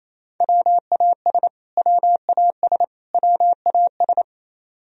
Morse code, heard as a keyed tone, about 27 wpm, 700 Hz. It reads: WAS WAS WAS
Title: Top 100 Words Repeated 28wpm